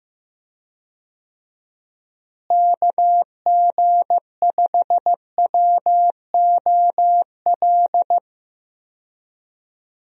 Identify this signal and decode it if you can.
Morse code, heard as a keyed tone, about 15 wpm, 700 Hz. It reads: KG5WOL
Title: USA Call Signs - 6 Characters - Repeated 3x - 15wpm